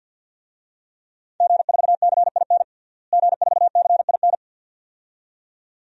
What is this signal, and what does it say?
Morse code, heard as a keyed tone, about 47 wpm, 700 Hz. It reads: G4CIN G4CIN
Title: DXCC-100 Call Signs - 5 Characters - Repeated 2x - 50wpm